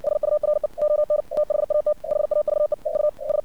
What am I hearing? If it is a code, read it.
Morse code, heard as a keyed tone, about 59 wpm, 600 Hz. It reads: SURE ON THAT HAVE EU S
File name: archived / CRSnet